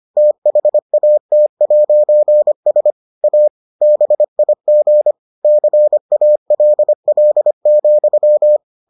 Morse code, heard as a keyed tone, about 25 wpm, 600 Hz. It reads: THAT'S A BIG CALL,